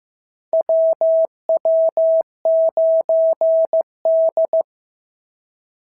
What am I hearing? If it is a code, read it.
Morse code, heard as a keyed tone, about 14 wpm, 650 Hz. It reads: WW9D
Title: SOTA QSO Practice 15wpm